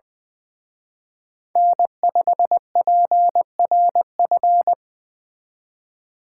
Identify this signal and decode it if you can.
Morse code, heard as a keyed tone, about 24 wpm, 700 Hz. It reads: N5PRF
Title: Speed Racing - USA 1x3 Call Signs 20wpm